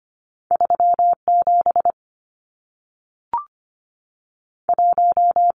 Morse code, heard as a keyed tone, about 25 wpm, 700 Hz. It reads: 37 1
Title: Numbers - 2-Digit 25wpm